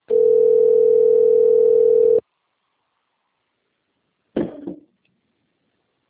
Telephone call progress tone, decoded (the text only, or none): ringback tone